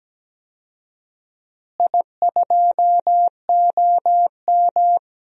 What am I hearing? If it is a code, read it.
Morse code, heard as a keyed tone, about 17 wpm, 700 Hz. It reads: I2OM
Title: DXCC-100 Call Signs - 4 Characters - Repeated 3x - 17wpm